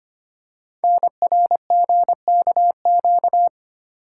Morse code, heard as a keyed tone, about 25 wpm, 700 Hz. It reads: NRGKQ
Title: Code Groups - 5 Letters - 25wpm